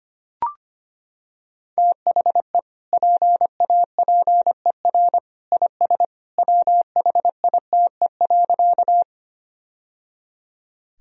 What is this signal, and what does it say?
Morse code, heard as a keyed tone, about 25 wpm, 700 Hz. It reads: THE PAPER IS WHITE.
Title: Sentences from Top 1000 Words - Review Part 1 - 25wpm